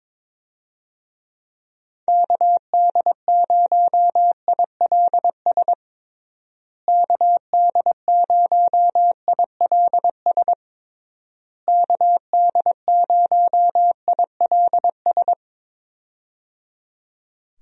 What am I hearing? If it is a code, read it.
Morse code, heard as a keyed tone, about 22 wpm, 700 Hz. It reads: KD0ILS KD0ILS KD0ILS
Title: USA Call Signs - 6 Characters - Repeated 3x - 22wpm